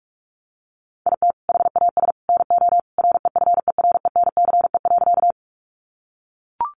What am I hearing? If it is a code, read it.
Morse code, heard as a keyed tone, about 44 wpm, 700 Hz. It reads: IT HAS NO REFERENCE.
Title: Sentences from Top 1200 Words 45wpm